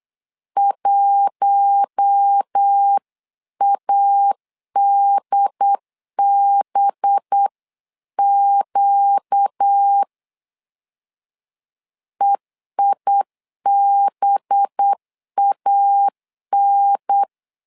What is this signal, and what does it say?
Morse code, heard as a keyed tone, about 8 wpm, 800 Hz. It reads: 1ADBQ EIBAN
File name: LEZIONI / 5 LBVC / 5 LB 8wpm 2816